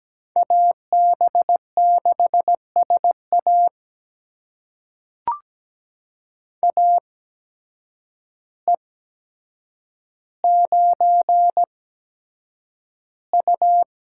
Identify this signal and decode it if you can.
Morse code, heard as a keyed tone, about 17 wpm, spaced as for 8 wpm, 700 Hz. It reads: AB6SA AE9U
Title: USA Call Signs - 8x Character Spacing - 17wpm